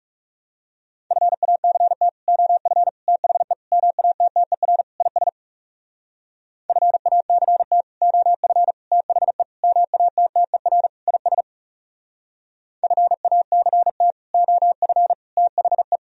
Morse code, heard as a keyed tone, about 40 wpm, 700 Hz. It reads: FACT OF THE MATTER IS FACT OF THE MATTER IS FACT OF THE